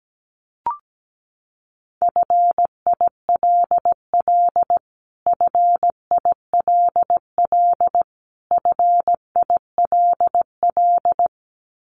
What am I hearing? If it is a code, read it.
Morse code, heard as a keyed tone, about 17 wpm, 700 Hz. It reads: FILL FILL FILL
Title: Top 500 Words Repeated 17wpm